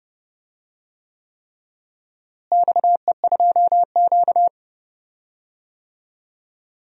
Morse code, heard as a keyed tone, about 32 wpm, 700 Hz. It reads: XE2Q
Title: DXCC-100 Call Signs - 4 Characters - Repeated 3x - 30wpm